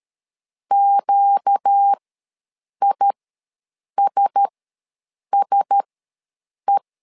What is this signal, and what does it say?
Morse code, heard as a keyed tone, about 13 wpm, spaced as for 7 wpm, 800 Hz. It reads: QISSE
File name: LEZIONI / 4 Q1S ws 12wpm 1827